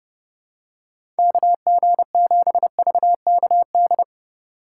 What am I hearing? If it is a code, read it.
Morse code, heard as a keyed tone, about 30 wpm, 700 Hz. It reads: KG7VKD
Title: USA Call Signs - 3x Character Spacing - 30wpm